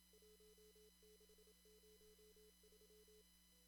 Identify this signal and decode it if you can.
Morse code, heard as a keyed tone, about 30 wpm, 440 Hz, with noise. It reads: 1603
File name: CW / kt5j wpx cw-op-k5pi